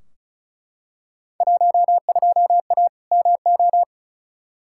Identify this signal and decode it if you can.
Morse code, heard as a keyed tone, about 36 wpm, 690 Hz. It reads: 12A MO
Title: Field Day Exchanges Encore - 35wpm